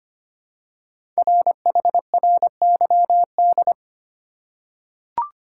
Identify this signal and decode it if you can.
Morse code, heard as a keyed tone, about 25 wpm, 700 Hz. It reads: RHRYD
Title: Code Groups - 5 Letters - 25wpm